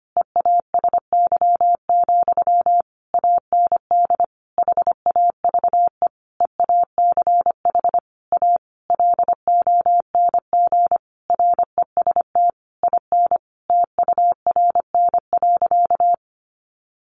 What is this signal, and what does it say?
Morse code, heard as a keyed tone, about 25 wpm, 700 Hz. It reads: EASY, AND HAVE EACH A LONG REST IN TURN.
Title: Sentences from Top 700 Words 25wpm